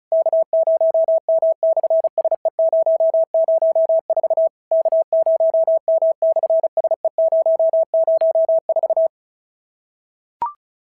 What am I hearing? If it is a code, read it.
Morse code, heard as a keyed tone, about 33 wpm, 650 Hz. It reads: K0M/SE004 K0M/SE004
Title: SOTA QSO Practice 35wpm